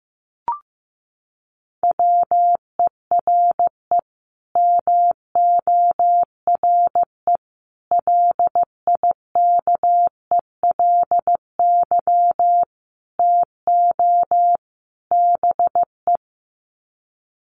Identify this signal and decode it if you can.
Morse code, heard as a keyed tone, about 14 wpm, 700 Hz. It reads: WERE MORE LIKELY TO BE
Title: Sets of 5 Words Encore2 15wpm